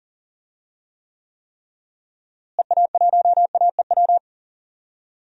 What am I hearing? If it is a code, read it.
Morse code, heard as a keyed tone, about 38 wpm, 700 Hz. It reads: EA1AEW
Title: DXCC-100 Call Signs - 6 Characters - Repeated 3x - 40wpm